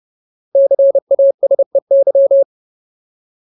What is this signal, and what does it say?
Morse code, heard as a keyed tone, about 30 wpm, 550 Hz. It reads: CASEY
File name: Ultimate Word List / 30WPM Ultimate